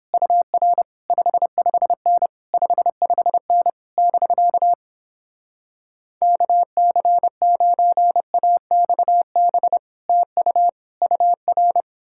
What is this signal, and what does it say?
Morse code, heard as a keyed tone, about 29 wpm, 700 Hz. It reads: UR 55N 55N <BK> KC9AXB TU UR